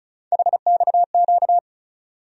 Morse code, heard as a keyed tone, about 36 wpm, 700 Hz. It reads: HXQ
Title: USA Call Sign Postfixes - 3 Letters 35wpm